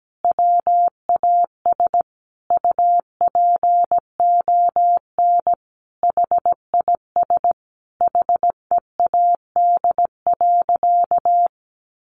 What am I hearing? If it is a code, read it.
Morse code, heard as a keyed tone, about 17 wpm, 700 Hz. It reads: WAS UPON HIS HEAD.
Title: Sentences from Top 1100 Words 17wpm